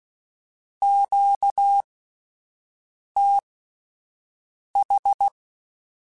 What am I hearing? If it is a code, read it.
Morse code, heard as a keyed tone, about 16 wpm, spaced as for 5 wpm, 800 Hz. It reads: QTH
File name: (spaced) / Q-Calls Spaced-mp3s / Q Calls sp Once @10